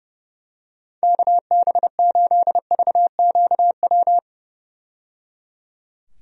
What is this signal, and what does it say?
Morse code, heard as a keyed tone, about 30 wpm, 700 Hz. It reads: KB8VQW
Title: USA Call Signs - 6 Characters - 30wpm